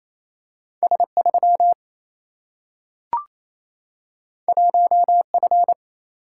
Morse code, heard as a keyed tone, about 28 wpm, 700 Hz. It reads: S 3 1 F
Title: Single-Letter Single-Number 28wpm